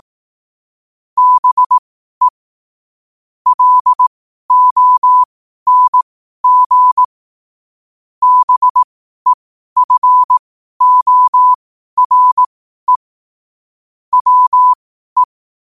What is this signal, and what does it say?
Morse code, heard as a keyed tone, about 18 wpm, spaced as for 12 wpm, 1000 Hz. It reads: BE LONG BEFORE WE
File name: qotdfiles / 12wpm